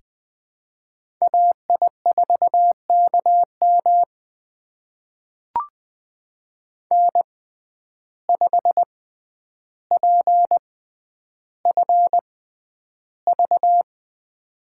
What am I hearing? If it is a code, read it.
Morse code, heard as a keyed tone, about 20 wpm, spaced as for 10 wpm, 700 Hz. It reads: AI4KM N5PFV
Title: USA Call Signs - 5 Characters - 6x Character Spacing - 20wpm